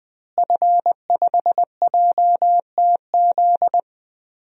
Morse code, heard as a keyed tone, about 20 wpm, 700 Hz. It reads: F5JTZ
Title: DXCC-100 Call Signs - 5 Characters - Repeated 2x - 20wpm